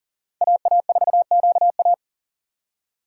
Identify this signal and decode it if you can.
Morse code, heard as a keyed tone, about 39 wpm, 700 Hz. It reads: AA4QA
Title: USA Call Signs - 2x Character Spacing - 40wpm